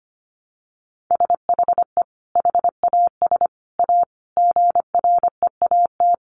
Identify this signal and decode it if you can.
Morse code, heard as a keyed tone, about 25 wpm, 700 Hz. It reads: SHE HAS A GREAT
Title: Sentences from Top 1300 Words 25wpm